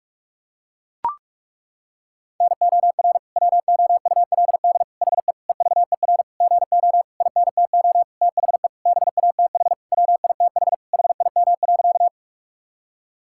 Morse code, heard as a keyed tone, about 41 wpm, 700 Hz. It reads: NOR WOULD HE EVER GO INTO THE BATH WITH HIM.
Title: Sentences from Top 1600 Words 45wpm